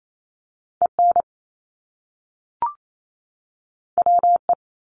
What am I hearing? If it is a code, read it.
Morse code, heard as a keyed tone, about 28 wpm, 700 Hz. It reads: EN WE
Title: Top Two Letter Combinations in Words 28wpm